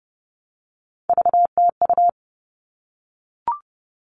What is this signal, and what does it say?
Morse code, heard as a keyed tone, about 30 wpm, 700 Hz. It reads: VTU
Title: All Cut Numbers - 3-Digit 30wpm